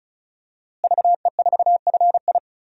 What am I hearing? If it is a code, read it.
Morse code, heard as a keyed tone, about 35 wpm, 700 Hz. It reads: VE4FI
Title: DXCC-100 Call Signs - 5 Characters - Repeated 3x - 35wpm